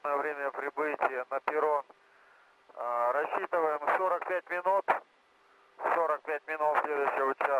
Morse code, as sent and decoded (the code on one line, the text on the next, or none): none